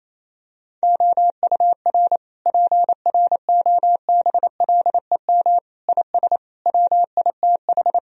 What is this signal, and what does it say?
Morse code, heard as a keyed tone, about 28 wpm, 700 Hz. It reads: OUR PROBLEM IS WITH
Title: Sentences from Top 1000 Words - Review Part 2 - 28wpm